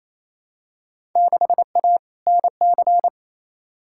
Morse code, heard as a keyed tone, about 28 wpm, 700 Hz. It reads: 6A NC
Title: Field Day Exchanges Encore - 28wpm